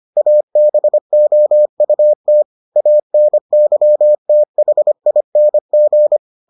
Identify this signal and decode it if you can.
Morse code, heard as a keyed tone, about 25 wpm, 600 Hz. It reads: ABOUT ANYTHING